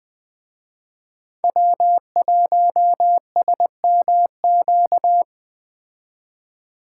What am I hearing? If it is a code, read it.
Morse code, heard as a keyed tone, about 20 wpm, 700 Hz. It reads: W1SMQ